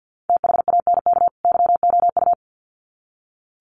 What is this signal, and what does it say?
Morse code, heard as a keyed tone, about 48 wpm, 700 Hz. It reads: THANK YOU
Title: Sets of 2 Words 50wpm